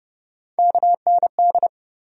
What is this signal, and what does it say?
Morse code, heard as a keyed tone, about 30 wpm, 700 Hz. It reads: KND
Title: USA Call Sign Postfixes - 3 Letters 30wpm